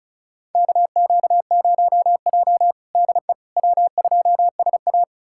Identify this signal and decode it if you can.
Morse code, heard as a keyed tone, about 36 wpm, 700 Hz. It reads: KQ0J DE W2SA
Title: USA Callsign DE Callsign - 4 Characters - 35wpm